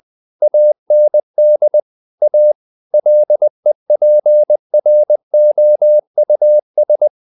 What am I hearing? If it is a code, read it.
Morse code, heard as a keyed tone, about 20 wpm, 600 Hz. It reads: AND A LEPROUS
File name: the old man and the sea part 2 20wpm